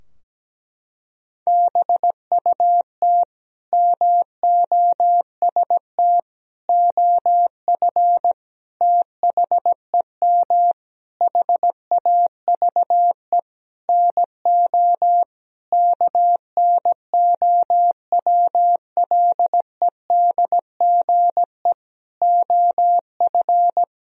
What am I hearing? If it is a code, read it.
Morse code, heard as a keyed tone, about 17 wpm, 700 Hz. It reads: BUT MOST OF THEM HAVE NO KNOWLEDGE OF
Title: Sentences from Top 1000 Words - Review Part 3 - 17wpm